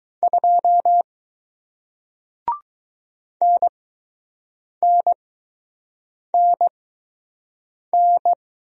Morse code, heard as a keyed tone, about 20 wpm, spaced as for 6 wpm, 700 Hz. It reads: 2 NNNN